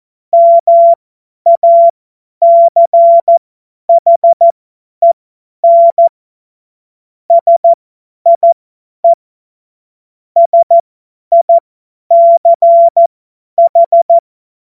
Morse code, heard as a keyed tone, about 14 wpm, spaced as for 10 wpm, 680 Hz. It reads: MACHEN SIE SICH
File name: Morsekurs / Witze / Witz 20 Februar 2024 50 F70